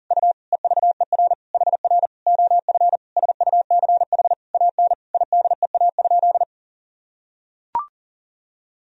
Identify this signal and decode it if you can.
Morse code, heard as a keyed tone, about 40 wpm, 700 Hz. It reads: U EVER HR OF SUCH AN IDEA?
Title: Sentences from Top 400 Words and Abbreviations 40wpm